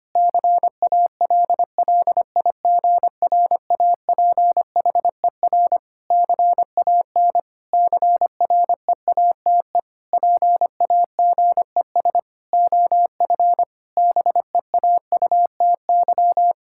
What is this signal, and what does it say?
Morse code, heard as a keyed tone, about 25 wpm, 700 Hz. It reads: CALLIGRAPHER CAN CREATE PAGES OF BEAUTY
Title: Pangram Challenge v2 - 25wpm